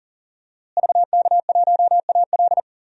Morse code, heard as a keyed tone, about 37 wpm, 700 Hz. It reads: VK1AL
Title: DXCC-100 Call Signs - 5 Characters - Repeated 3x - 40wpm